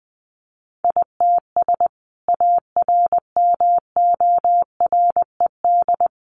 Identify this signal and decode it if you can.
Morse code, heard as a keyed tone, about 20 wpm, 700 Hz. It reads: ITS ARMORED
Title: A Sound Of Thunder 20wpm